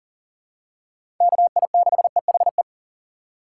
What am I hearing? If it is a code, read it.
Morse code, heard as a keyed tone, about 40 wpm, 700 Hz. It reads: KI6EHE
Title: USA Call Signs - 6 Characters - 6x Character Spacing - 40wpm